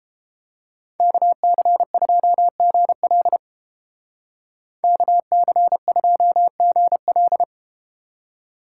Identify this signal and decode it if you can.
Morse code, heard as a keyed tone, about 33 wpm, 700 Hz. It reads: KC2GL KC2GL